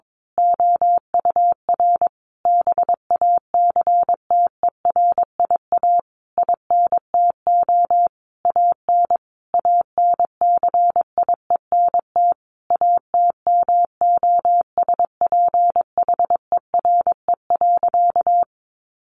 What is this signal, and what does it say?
Morse code, heard as a keyed tone, about 22 wpm, 700 Hz. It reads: OUR BACTERIA INTO AN ANCIENT ATMOSPHERE.
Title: A Sound Of Thunder 22wpm